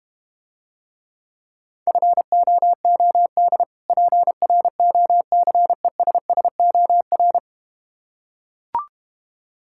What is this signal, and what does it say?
Morse code, heard as a keyed tone, about 32 wpm, 700 Hz. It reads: FOOD PROCESSOR